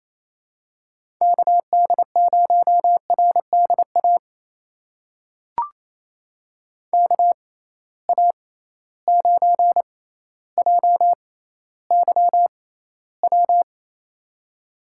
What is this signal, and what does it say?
Morse code, heard as a keyed tone, about 27 wpm, spaced as for 14 wpm, 700 Hz. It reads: KD0RDA KA9JYW
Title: USA Call Signs - 6 Characters - 6x Character Spacing - 28wpm